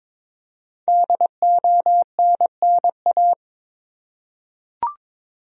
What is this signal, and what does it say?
Morse code, heard as a keyed tone, about 22 wpm, 700 Hz. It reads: DONNA
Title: USA Common Names - 22wpm